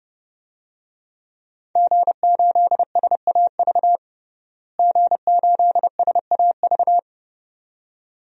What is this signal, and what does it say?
Morse code, heard as a keyed tone, about 32 wpm, 700 Hz. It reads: G8SAV G8SAV
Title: DXCC-100 Call Signs - 5 Characters - Repeated 2x - 30wpm